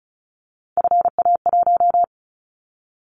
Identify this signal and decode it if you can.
Morse code, heard as a keyed tone, about 36 wpm, 700 Hz. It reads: FA1
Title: Callsign Prefixes for Top 50 DXCC - 35wpm